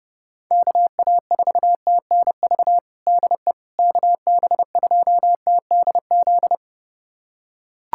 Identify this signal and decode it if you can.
Morse code, heard as a keyed tone, about 32 wpm, 700 Hz. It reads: KA4TNV DE KB2TDZ
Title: USA Callsign DE Callsign - 6 Characters - 30wpm